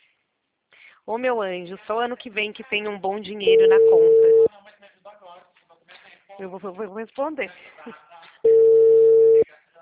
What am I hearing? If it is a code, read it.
Ringback tone (China, 450 Hz), with speech.
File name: out- -2011- - - .7724